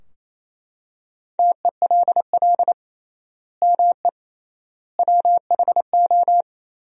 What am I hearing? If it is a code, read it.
Morse code, heard as a keyed tone, about 28 wpm, 700 Hz. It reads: TELL ME WHO
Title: 3x Word-Spacing - Sentences from Top 200 Words - 28wpm